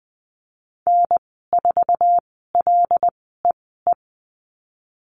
Morse code, heard as a keyed tone, about 20 wpm, spaced as for 15 wpm, 700 Hz. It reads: N4LEE
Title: 088 Lesson 15g - Number 5 - Call Signs 20wpm